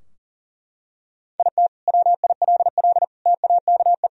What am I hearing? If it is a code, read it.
Morse code, heard as a keyed tone, about 37 wpm, 700 Hz. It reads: IT WILL TAKE
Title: Sentences from Top 1000 Words - Review Part 4 - 40wpm